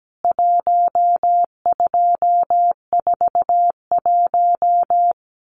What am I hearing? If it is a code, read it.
Morse code, heard as a keyed tone, about 17 wpm, 700 Hz. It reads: 1241
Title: CWT Exchange 17wpm